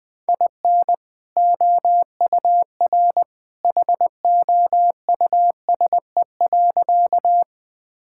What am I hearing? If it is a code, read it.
Morse code, heard as a keyed tone, about 20 wpm, 700 Hz. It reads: IN OUR HOUSE.
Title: Sentences with 2-Word Phrases - 20wpm